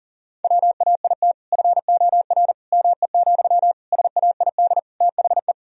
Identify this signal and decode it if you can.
Morse code, heard as a keyed tone, about 40 wpm, 700 Hz. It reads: WAIT FOR ME, SAID THE
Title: Sentences from Top 1000 Words - Review Part 4 - 40wpm